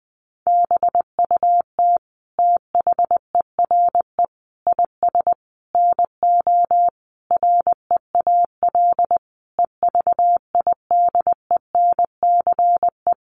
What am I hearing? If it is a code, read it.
Morse code, heard as a keyed tone, about 20 wpm, 700 Hz. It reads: BUT THERE IS NO REAL EVIDENCE
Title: Sentences from Top 1300 Words 20wpm